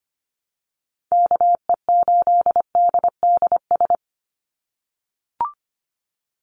Morse code, heard as a keyed tone, about 25 wpm, 700 Hz. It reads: KE8DDS
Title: USA Call Signs 25wpm